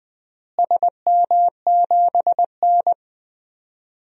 Morse code, heard as a keyed tone, about 20 wpm, 700 Hz. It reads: SM7N
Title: DXCC-100 Call Signs - 4 Characters - Repeated 2x - 20wpm